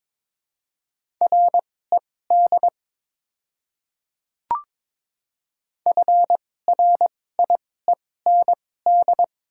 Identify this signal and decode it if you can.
Morse code, heard as a keyed tone, about 22 wpm, spaced as for 16 wpm, 700 Hz. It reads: RED FRIEND
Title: 2x Character Spacing - Top 500 Words - 22wpm